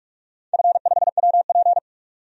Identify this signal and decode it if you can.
Morse code, heard as a keyed tone, about 41 wpm, 700 Hz. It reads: F5WP
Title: DXCC-100 Call Signs - 4 Characters - 45wpm